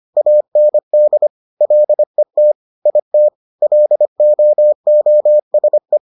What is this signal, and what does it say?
Morse code, heard as a keyed tone, about 25 wpm, 600 Hz. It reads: AND LET IT LOOSE